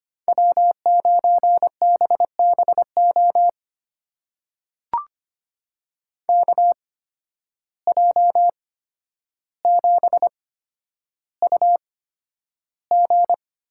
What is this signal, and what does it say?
Morse code, heard as a keyed tone, about 25 wpm, spaced as for 11 wpm, 700 Hz. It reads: W9BBO KJ7UG
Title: USA Call Signs - 5 Characters - 8x Character Spacing - 25wpm